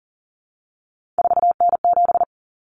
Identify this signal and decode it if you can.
Morse code, heard as a keyed tone, about 40 wpm, 700 Hz. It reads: 4N7
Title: Common Cut Numbers - 3-Digit 40wpm